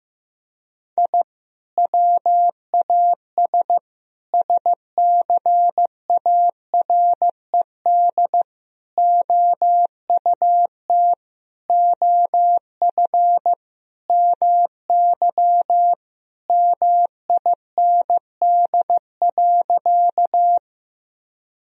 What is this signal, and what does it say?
Morse code, heard as a keyed tone, about 15 wpm, 700 Hz. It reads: I WAS SCARED OUT OF MY MIND.
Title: Sentences from Top 1000 Words - Review Part 4 - 15wpm